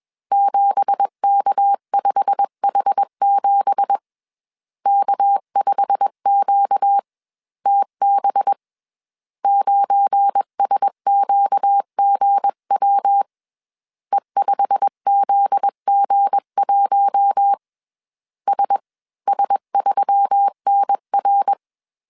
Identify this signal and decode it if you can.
Morse code, heard as a keyed tone, about 21 wpm, spaced as for 19 wpm, 800 Hz. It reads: 7K5H7 K5QTB 9SQGW E5ZG1 SS3NR
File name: CW / DET / 1 9 1500C S C 20wpm 2310